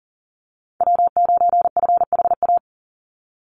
Morse code, heard as a keyed tone, about 40 wpm, 700 Hz. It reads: W9FHA
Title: NA SOTA Chaser Callsigns 40wpm